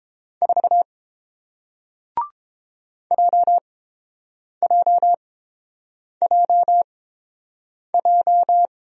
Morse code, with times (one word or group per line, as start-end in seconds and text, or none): 0.4-0.8 4
3.1-8.7 JJJJ